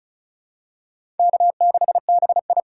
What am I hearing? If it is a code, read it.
Morse code, heard as a keyed tone, about 36 wpm, 700 Hz. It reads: K6BI
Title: USA Call Signs - 3x Character Spacing - 35wpm